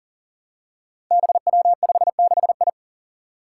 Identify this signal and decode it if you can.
Morse code, heard as a keyed tone, about 40 wpm, 700 Hz. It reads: BW56I
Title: Five Challenging Letter-Number Combinations 40wpm